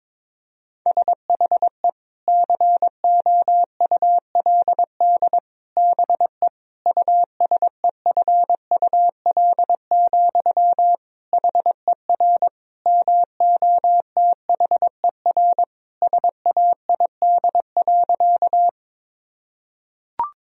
Morse code, heard as a keyed tone, about 22 wpm, 700 Hz. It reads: SHE COULD BE USEFUL, HER MOTHER SAID.
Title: Sentences from Top 1000 Words - Review Part 4 - 22wpm